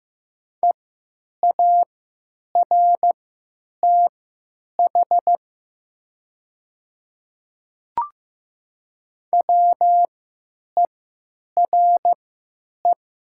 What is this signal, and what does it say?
Morse code, heard as a keyed tone, about 15 wpm, spaced as for 8 wpm, 700 Hz. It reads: EARTH WERE
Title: 3x Character Spacing - Top 200 Words - 15wpm